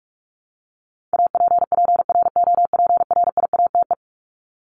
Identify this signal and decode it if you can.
Morse code, heard as a keyed tone, about 44 wpm, 700 Hz. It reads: APPROPRIATE
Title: Top 751-1000 Words 45wpm